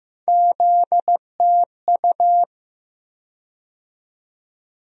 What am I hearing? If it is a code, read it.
Morse code, heard as a keyed tone, about 15 wpm, 700 Hz. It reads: ZTU
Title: USA Call Sign Postfixes - 3 Letters 15wpm